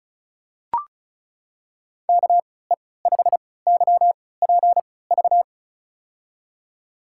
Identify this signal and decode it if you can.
Morse code, heard as a keyed tone, about 34 wpm, spaced as for 20 wpm, 700 Hz. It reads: KE5YPV
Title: USA Call Signs - 6 Characters - 3x Character Spacing - 35wpm